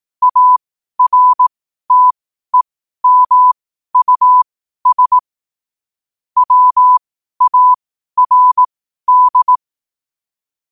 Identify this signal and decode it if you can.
Morse code, heard as a keyed tone, about 18 wpm, spaced as for 12 wpm, 1000 Hz. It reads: ARTEMUS WARD
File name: qotdfiles / 12wpm